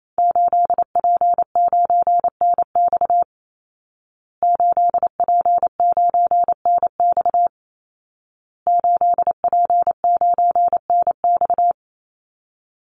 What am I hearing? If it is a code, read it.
Morse code, heard as a keyed tone, about 28 wpm, 700 Hz. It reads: 8P9NX 8P9NX 8P9NX
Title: Call Signs Repeated 2x - 28wpm